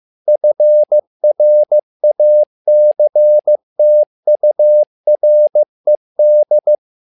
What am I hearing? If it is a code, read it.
Morse code, heard as a keyed tone, about 15 wpm, 600 Hz. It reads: FRACTURED
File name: learning center / tower trouble / tower trouble 15wpm